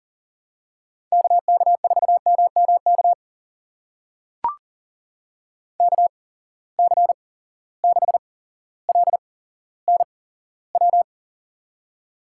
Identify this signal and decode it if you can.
Morse code, heard as a keyed tone, about 40 wpm, spaced as for 17 wpm, 700 Hz. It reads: KK4MMK KC6LNW
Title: USA Call Signs - 6 Characters - 8x Character Spacing - 40wpm